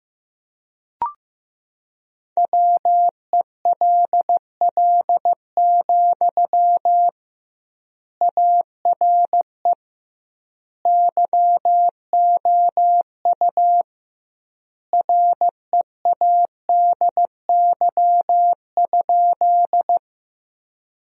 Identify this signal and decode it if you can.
Morse code, heard as a keyed tone, about 15 wpm, 700 Hz. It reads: WELL, ARE YOU READY?
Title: 2x Word-Spacing - Sentences from Top 400 Words - 15wpm